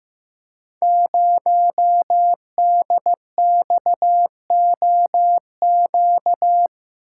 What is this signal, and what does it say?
Morse code, heard as a keyed tone, about 15 wpm, 700 Hz. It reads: 0DXOQ
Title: Code Groups - 5 Letter:Numbers - 15wpm